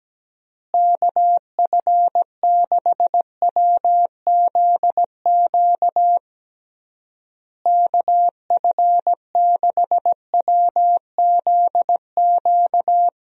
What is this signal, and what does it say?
Morse code, heard as a keyed tone, about 17 wpm, 700 Hz. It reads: KF6WZQ KF6WZQ
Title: USA Call Signs - 6 Characters - Repeated 3x - 17wpm